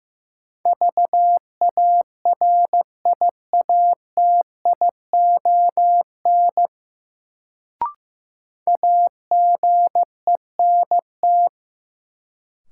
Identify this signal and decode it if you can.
Morse code, heard as a keyed tone, about 15 wpm, 700 Hz. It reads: VARIATION AGENT
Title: Sentences from Top 1600 Words - New Word Drill Alt - 15wpm